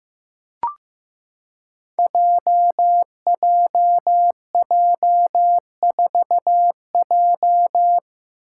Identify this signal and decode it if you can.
Morse code, heard as a keyed tone, about 15 wpm, 700 Hz. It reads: JJJ4J
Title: Five Challenging Letter-Number Combinations 15wpm